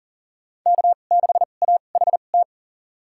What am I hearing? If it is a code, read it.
Morse code, heard as a keyed tone, about 40 wpm, spaced as for 29 wpm, 700 Hz. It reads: K6AHT
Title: USA Call Signs - 5 Characters - 2x Character Spacing - 40wpm